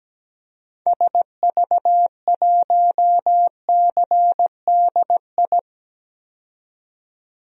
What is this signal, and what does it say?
Morse code, heard as a keyed tone, about 17 wpm, 700 Hz. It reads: SV1CDI
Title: DXCC-100 Call Signs - 6 Characters - 17wpm